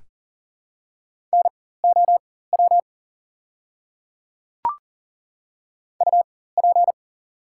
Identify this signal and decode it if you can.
Morse code, heard as a keyed tone, about 40 wpm, spaced as for 19 wpm, 700 Hz. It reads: NOW UP
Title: 4x Character Spacing - Top 100 Words - 40wpm